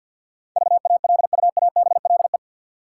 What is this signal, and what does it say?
Morse code, heard as a keyed tone, about 50 wpm, 700 Hz. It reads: VALUABLE
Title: Sentences from Top 1100 Words - New Word Drill - 50wpm